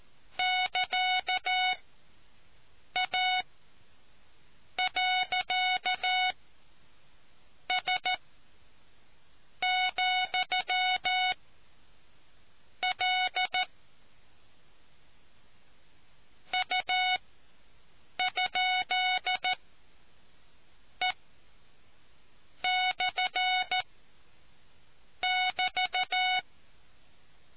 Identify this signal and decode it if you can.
Morse code, heard as a keyed tone, about 14 wpm, 740 Hz. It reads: <KA> A . S , L U ? E / <BT>